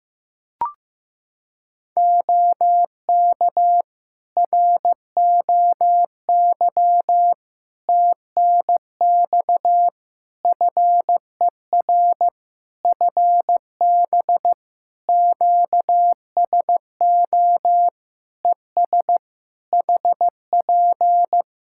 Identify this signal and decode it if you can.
Morse code, heard as a keyed tone, about 15 wpm, 700 Hz. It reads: OK ROY TNX FER FB QSO ES HP
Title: Practice QSO - Exchange 4 - Ending - 15wpm